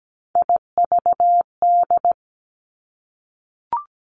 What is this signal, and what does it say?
Morse code, heard as a keyed tone, about 17 wpm, 700 Hz. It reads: IVD
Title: Three Challenging Letter-Number Combinations 17wpm